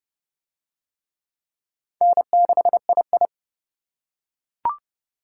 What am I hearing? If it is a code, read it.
Morse code, heard as a keyed tone, about 25 wpm, 700 Hz. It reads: N6II